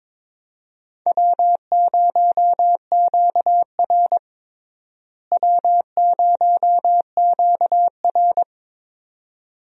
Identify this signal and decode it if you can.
Morse code, heard as a keyed tone, about 22 wpm, 700 Hz. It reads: W0QR W0QR
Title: USA Call Signs - 4 Characters - Repeated 2x - 22wpm